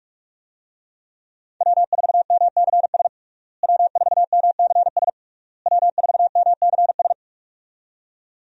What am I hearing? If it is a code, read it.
Morse code, heard as a keyed tone, about 42 wpm, 700 Hz. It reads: W4MCS W4MCS W4MCS
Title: USA Call Signs - 5 Characters - Repeated 3x - 45wpm